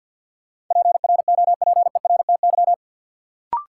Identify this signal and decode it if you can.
Morse code, heard as a keyed tone, about 49 wpm, 700 Hz. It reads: PROPERTY
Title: Sentences from Top 1000 Words - New Word Drill - 50wpm